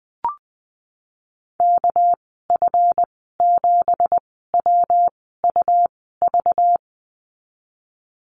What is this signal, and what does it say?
Morse code, heard as a keyed tone, about 20 wpm, spaced as for 15 wpm, 700 Hz. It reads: KF7WUV
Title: 213 Lesson 34g - Number 8 - Call Signs 20wpm